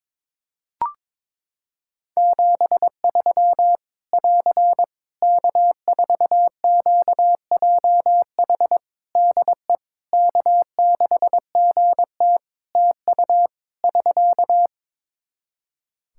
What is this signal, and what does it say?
Morse code, heard as a keyed tone, about 22 wpm, 700 Hz. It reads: 73 <AR> K4QJH DE K6GT TU <SK>
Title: Practice QSO - Exchange 4 - Ending - 22wpm